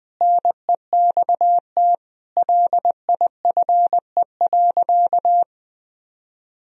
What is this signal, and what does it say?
Morse code, heard as a keyed tone, about 20 wpm, 700 Hz. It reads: NEXT LIFE.
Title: Sentences from Top 1000 Words - Review Part 1 - 20wpm